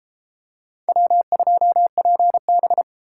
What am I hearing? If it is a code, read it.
Morse code, heard as a keyed tone, about 31 wpm, 700 Hz. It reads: W2PB